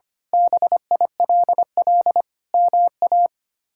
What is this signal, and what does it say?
Morse code, heard as a keyed tone, about 25 wpm, 700 Hz. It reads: BILL MA
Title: K1USN SST QSO Practice - 25wpm